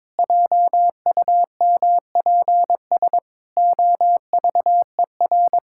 Morse code, heard as a keyed tone, about 21 wpm, 700 Hz. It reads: JUMPS OVER
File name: practice / 22wpm / Sets of 8 Words - 22wpm